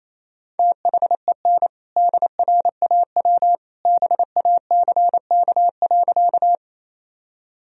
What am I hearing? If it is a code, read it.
Morse code, heard as a keyed tone, about 28 wpm, 700 Hz. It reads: THEN DRAW BACK.
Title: Sentences from Top 1000 Words - Review Part 1 - 28wpm